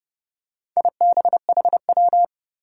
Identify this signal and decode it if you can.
Morse code, heard as a keyed tone, about 30 wpm, 700 Hz. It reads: IBHW
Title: Four Challenging Letter-Number Combinations 30wpm